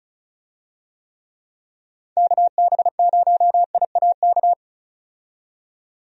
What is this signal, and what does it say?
Morse code, heard as a keyed tone, about 34 wpm, 700 Hz. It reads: KB0IAK
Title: USA Call Signs - 6 Characters - 35wpm